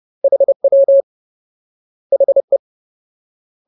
Morse code, heard as a keyed tone, about 30 wpm, 550 Hz. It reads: HW HE
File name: Ultimate Word List / 30WPM Ultimate